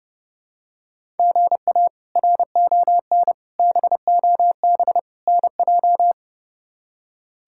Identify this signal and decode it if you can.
Morse code, heard as a keyed tone, about 32 wpm, 700 Hz. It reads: GA RON BOB NJ
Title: K1USN SST QSO Practice - 30wpm